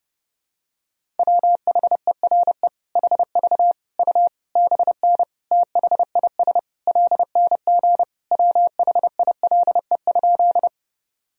Morse code, heard as a keyed tone, about 30 wpm, 700 Hz. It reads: WHERE HV U BN THIS LNG WHILE?
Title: Sentences from Top 300 Words and Abbreviations 30wpm